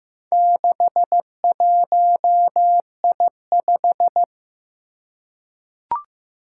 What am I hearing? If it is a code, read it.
Morse code, heard as a keyed tone, about 15 wpm, 700 Hz. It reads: 61I5
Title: Four Challenging Letter-Number Combinations 15wpm